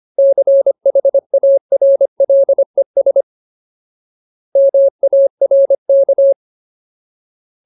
Morse code, heard as a keyed tone, about 25 wpm, 550 Hz. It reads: CHARLES MARK
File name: Ultimate Word List / 25WPM Ultimate